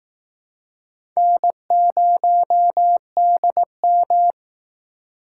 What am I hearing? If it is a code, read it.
Morse code, heard as a keyed tone, about 18 wpm, 700 Hz. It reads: N0DM
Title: Speed Racing - USA 1x2 Call Signs 15wpm